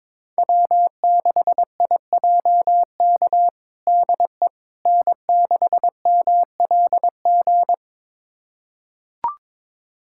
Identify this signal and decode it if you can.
Morse code, heard as a keyed tone, about 22 wpm, 700 Hz. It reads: W6IJK DE N6MLG
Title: USA Callsign DE Callsign - 5 Characters - 22wpm